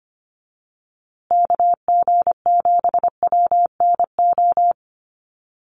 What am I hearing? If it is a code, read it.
Morse code, heard as a keyed tone, about 25 wpm, 700 Hz. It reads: KG7WNO
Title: USA Call Signs Repeated 2x 25wpm